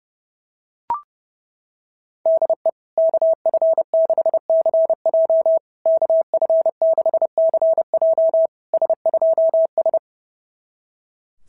Morse code, heard as a keyed tone, about 30 wpm, 650 Hz. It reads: DE KF6CJ KF6CJ S2S
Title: SOTA QSO Practice 30wpm